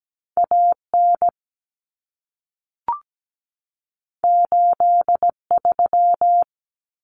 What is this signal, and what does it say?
Morse code, heard as a keyed tone, about 17 wpm, 700 Hz. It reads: AN 83
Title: Common Cut Numbers - 2-Digit 17wpm